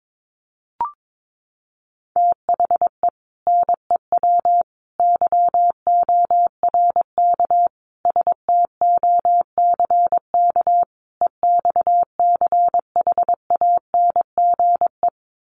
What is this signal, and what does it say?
Morse code, heard as a keyed tone, about 22 wpm, 700 Hz. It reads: THE NEW YORK STOCK EXCHANGE
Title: Sets of 5 Words 22wpm